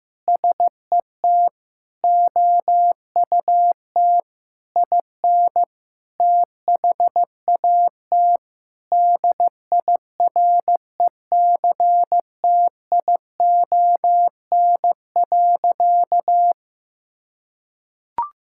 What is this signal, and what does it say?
Morse code, heard as a keyed tone, about 15 wpm, 700 Hz. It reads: SET OUT IN THAT DIRECTION.
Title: Sentences from Top 1000 Words - Review Part 5 - 15wpm